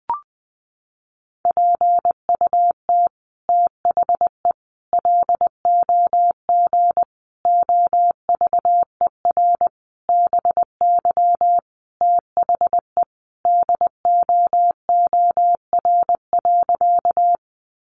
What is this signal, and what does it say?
Morse code, heard as a keyed tone, about 20 wpm, 700 Hz. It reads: PUT THE LOG OVER BY THE DOOR.
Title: Sentences from Top 1400 Words 20wpm